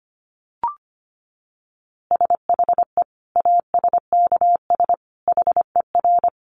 Morse code, heard as a keyed tone, about 25 wpm, 700 Hz. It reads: SHE ASKS HER
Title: Easy Phrases 25wpm